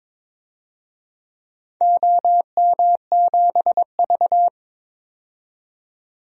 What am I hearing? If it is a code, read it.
Morse code, heard as a keyed tone, about 22 wpm, 700 Hz. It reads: OM7V
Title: DXCC-100 Call Signs - 4 Characters - 22wpm